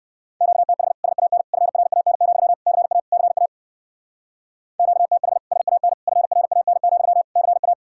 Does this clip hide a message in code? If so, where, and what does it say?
Yes: Morse code from 0.4 to 7.7 s, heard as a keyed tone, about 57 wpm, 700 Hz. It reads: QTH SAN FRAN, CA CA QTH SAN FRAN, CA